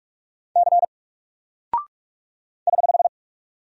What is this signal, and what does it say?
Morse code, heard as a keyed tone, about 46 wpm, 700 Hz. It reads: C <HH>
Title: Single Character (Letter:Number:Punctuation:Prosign) 45wpm